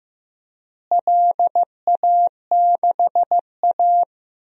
Morse code, heard as a keyed tone, about 15 wpm, 700 Hz. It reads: LA6A
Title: DXCC-100 Call Signs - 4 Characters - Repeated 3x - 15wpm